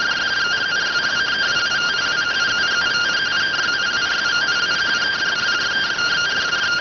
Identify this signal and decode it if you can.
Two-tone teleprinter signal, about 170 Hz shift.